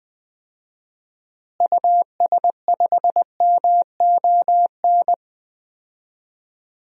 Morse code, heard as a keyed tone, about 20 wpm, 700 Hz. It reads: US5MON
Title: DXCC-100 Call Signs - 6 Characters - Repeated 3x - 20wpm